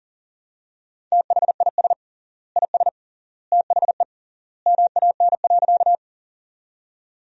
Morse code, heard as a keyed tone, about 40 wpm, 700 Hz. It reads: THIS IS THE MAN.
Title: 3x Word-Spacing - Sentences from Top 200 Words - 40wpm